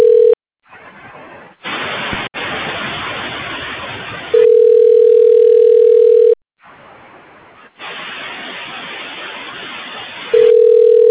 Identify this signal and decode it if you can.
Ringback tone (North America), 440 + 480 Hz.
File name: recordings / RE5b40ecb c96e9d535c3c1f45d43